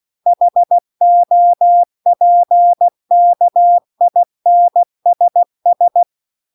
Morse code, heard as a keyed tone, about 16 wpm, 700 Hz. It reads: HOPKINSS